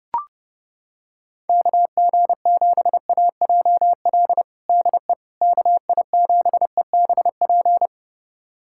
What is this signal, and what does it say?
Morse code, heard as a keyed tone, about 30 wpm, 700 Hz. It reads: KG7AJL DE KI7EBP
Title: USA Callsign DE Callsign - 6 Characters - 30wpm